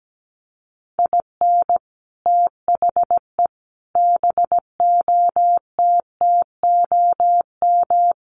Morse code, heard as a keyed tone, about 17 wpm, 700 Hz. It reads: IN THE BOTTOM
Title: Sets of 5 Words Encore 17wpm